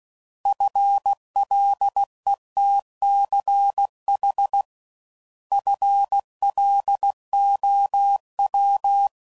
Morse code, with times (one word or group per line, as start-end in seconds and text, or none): 0.5-4.6 FLETCH
5.5-9.1 FLOW